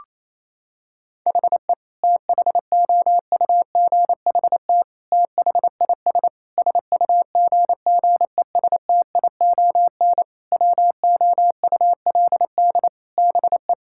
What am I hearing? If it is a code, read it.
Morse code, heard as a keyed tone, about 28 wpm, 700 Hz. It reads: HE THOUGHT THIS SUGGESTION WOULD BE
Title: Sentences from Top 1600 Words 28wpm